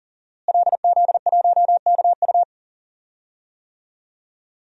Morse code, heard as a keyed tone, about 40 wpm, 700 Hz. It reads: LZ1KU
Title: DXCC-100 Call Signs - 5 Characters - Repeated 3x - 40wpm